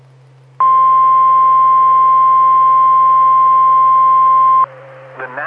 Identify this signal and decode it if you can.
Emergency Alert System (SAME) tones (the 1050 Hz attention tone). It is heard from 0.5 to 4.7 s.
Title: Saline, AK SKYWARN Repeater System - W5RHS